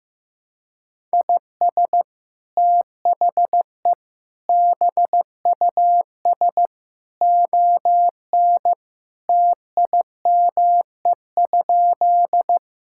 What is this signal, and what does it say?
Morse code, heard as a keyed tone, about 16 wpm, 700 Hz. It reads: IS THE BUS ON TIME?
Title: Sentences from Top 1000 Words - Review Part 6 - 15wpm